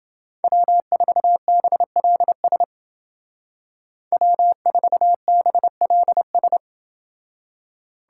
Morse code, heard as a keyed tone, about 30 wpm, 700 Hz. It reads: W4BLS W4BLS